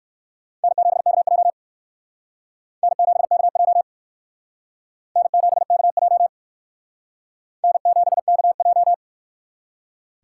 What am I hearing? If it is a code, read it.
Morse code, heard as a keyed tone, about 59 wpm, 700 Hz. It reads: N7KJ N7KJ N7KJ N7KJ